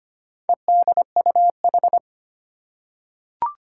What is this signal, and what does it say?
Morse code, heard as a keyed tone, about 25 wpm, 700 Hz. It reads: EDUH
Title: Four Challenging Letter-Number Combinations 25wpm